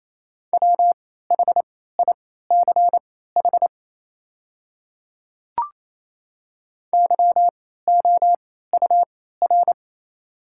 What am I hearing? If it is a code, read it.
Morse code, heard as a keyed tone, about 27 wpm, spaced as for 16 wpm, 700 Hz. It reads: WHICH YOUR
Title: 3x Character Spacing - Top 200 Words - 28wpm